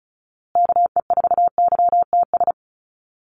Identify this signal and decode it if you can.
Morse code, heard as a keyed tone, about 34 wpm, 700 Hz. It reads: KE4YTS
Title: USA Call Signs Repeated 3x 35wpm